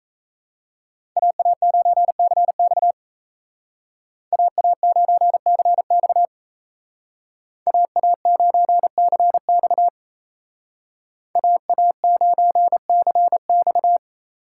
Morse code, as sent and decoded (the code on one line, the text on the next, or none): .- .- ----. -.-. -..- / .- .- ----. -.-. -..- / .- .- ----. -.-. -..- / .- .- ----. -.-. -..-
AA9CX AA9CX AA9CX AA9CX